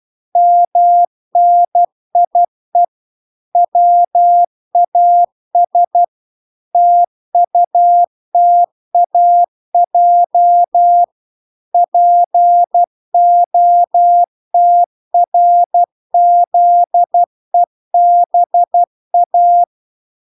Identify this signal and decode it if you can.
Morse code, heard as a keyed tone, about 12 wpm, 700 Hz. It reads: MNIE WAS TUTAJ POTRZEBA